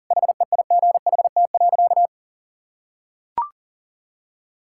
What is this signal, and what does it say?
Morse code, heard as a keyed tone, about 37 wpm, 700 Hz. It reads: HEIGHT.
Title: Sentences from Top 1600 Words 40wpm